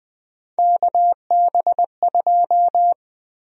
Morse code, heard as a keyed tone, about 20 wpm, 700 Hz. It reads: KB2
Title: USA Call Sign Prefixes - 20wpm